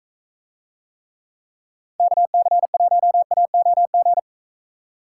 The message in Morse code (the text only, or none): KC1AOG